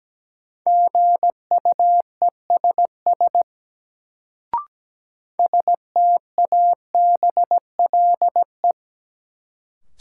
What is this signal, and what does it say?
Morse code, heard as a keyed tone, about 17 wpm, 700 Hz. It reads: GUESS STABLE
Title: Sentences from Top 1500 Words - New Word Drill - 17wpm